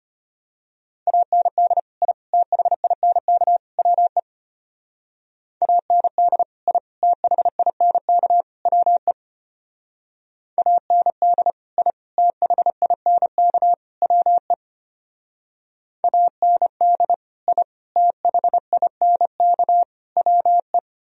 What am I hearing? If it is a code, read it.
Morse code, heard as a keyed tone, about 32 wpm, 700 Hz. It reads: AND I THINK WE AND I THINK WE AND I THINK WE AND I THINK WE